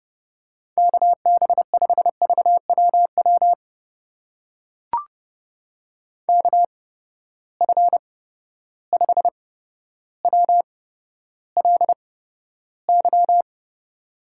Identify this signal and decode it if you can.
Morse code, heard as a keyed tone, about 30 wpm, spaced as for 13 wpm, 700 Hz. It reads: KB5VWW KF5WLY
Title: USA Call Signs - 6 Characters - 8x Character Spacing - 30wpm